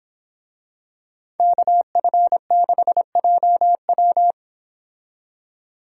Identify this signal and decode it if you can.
Morse code, heard as a keyed tone, about 26 wpm, 700 Hz. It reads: KF6JW
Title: Speed Racing - USA 2x2 Call Signs 22wpm